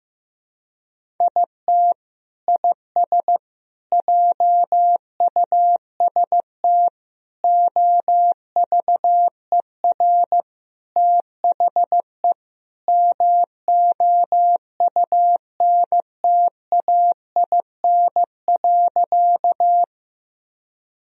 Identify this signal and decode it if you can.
Morse code, heard as a keyed tone, about 16 wpm, 700 Hz. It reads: IT IS JUST OVER THE MOUNTAIN.
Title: Sentences from Top 1000 Words - Review Part 1 - 15wpm